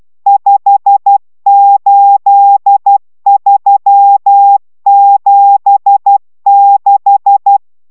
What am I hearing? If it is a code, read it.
Morse code, heard as a keyed tone, about 12 wpm, 800 Hz. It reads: 58376